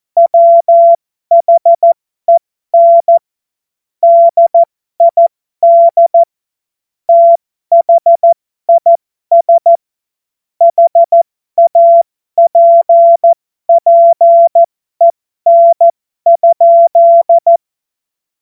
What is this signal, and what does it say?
Morse code, heard as a keyed tone, about 14 wpm, spaced as for 12 wpm, 680 Hz. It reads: WHEN DID THIS HAPPEN?
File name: Morsekurs / Witze